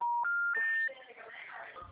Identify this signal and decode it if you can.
Special information tone, with speech.